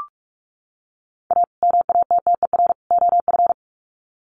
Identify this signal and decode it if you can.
Morse code, heard as a keyed tone, about 44 wpm, 700 Hz. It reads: A MATTER OF
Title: Sets of 3 Words 45wpm